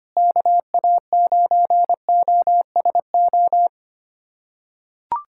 Morse code, heard as a keyed tone, about 25 wpm, 700 Hz. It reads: KA9OSO
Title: USA Callsign DE Callsign - 6 Characters - 25wpm